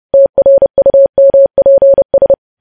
Morse code, heard as a keyed tone, about 30 wpm, 550 Hz. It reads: TRUMPS